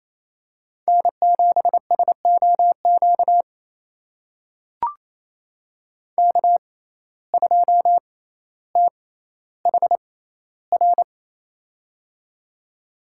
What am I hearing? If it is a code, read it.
Morse code, heard as a keyed tone, about 28 wpm, spaced as for 14 wpm, 700 Hz. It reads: N7SOQ K2THR
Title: USA Call Signs - 5 Characters - 6x Character Spacing - 28wpm